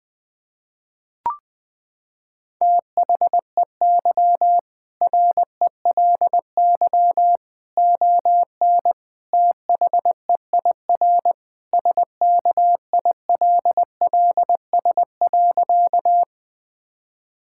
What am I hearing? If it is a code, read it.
Morse code, heard as a keyed tone, about 20 wpm, 700 Hz. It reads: THEY RELY ON THEIR SKILLS.
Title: Sentences with 2-Word Phrases - 20wpm